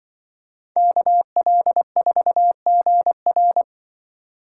Morse code, heard as a keyed tone, about 25 wpm, 700 Hz. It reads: KL4GR
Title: Speed Racing - USA 2x2 Call Signs 20wpm